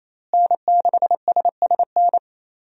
Morse code, heard as a keyed tone, about 28 wpm, 700 Hz. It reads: N6SSN
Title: USA Call Signs - 5 Characters - Repeated 3x - 28wpm